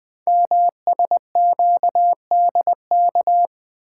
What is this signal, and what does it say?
Morse code, heard as a keyed tone, about 20 wpm, 700 Hz. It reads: MSQDK
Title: Code Groups - 5 Letters - 20wpm